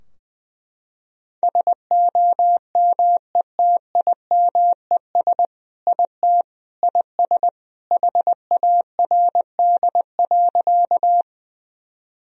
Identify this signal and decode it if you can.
Morse code, heard as a keyed tone, about 21 wpm, 700 Hz. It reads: SOMETIMES IT IS HARD.
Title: Sentences from Top 1000 Words - Review Part 3 - 20wpm